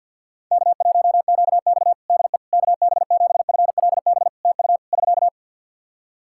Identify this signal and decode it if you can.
Morse code, heard as a keyed tone, about 51 wpm, 700 Hz. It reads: K1QX DE KB7FLB TU <SK>
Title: Practice QSO - Exchange 4 - Ending - 50wpm